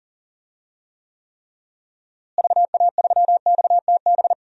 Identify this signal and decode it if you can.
Morse code, heard as a keyed tone, about 40 wpm, 700 Hz. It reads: VA3XTB
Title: DXCC-100 Call Signs - 6 Characters - Repeated 3x - 40wpm